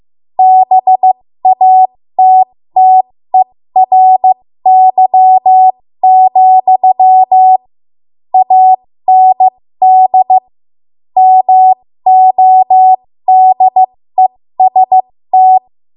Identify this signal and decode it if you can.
Morse code, heard as a keyed tone, about 15 wpm, spaced as for 13 wpm, 750 Hz. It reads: BATTERY, AND MODEST
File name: Morse / 13WPM